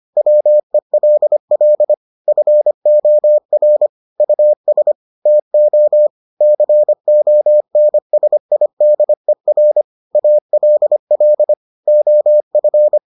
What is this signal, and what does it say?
Morse code, heard as a keyed tone, about 25 wpm, 600 Hz. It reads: WELL FOR US TO CONSIDER ALL OF